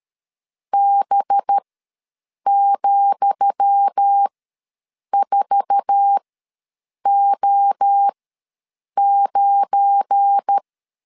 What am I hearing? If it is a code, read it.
Morse code, heard as a keyed tone, about 13 wpm, spaced as for 7 wpm, 800 Hz. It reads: B,4O9
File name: LEZIONI / 10 SIGNS / 10D COMA ws 12wpm 3618